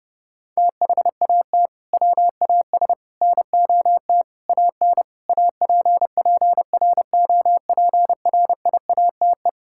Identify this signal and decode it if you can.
Morse code, heard as a keyed tone, about 30 wpm, 700 Hz. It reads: THAT WAS NOT AN APPROPRIATE
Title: Sentences from Top 1000 Words - Review Part 5 - 30wpm